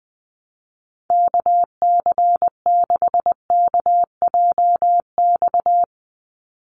Morse code, heard as a keyed tone, about 20 wpm, 700 Hz. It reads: KC6KJX
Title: USA Call Signs 20wpm